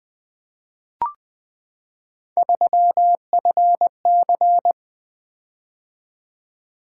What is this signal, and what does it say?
Morse code, heard as a keyed tone, about 20 wpm, 700 Hz. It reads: 3FC
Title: Three Letter-Number Combinations - 20wpm